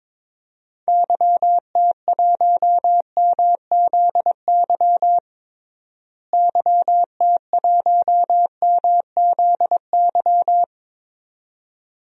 Morse code, heard as a keyed tone, about 22 wpm, 700 Hz. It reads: YT1MZY YT1MZY
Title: DXCC-100 Call Signs - 6 Characters - Repeated 2x - 22wpm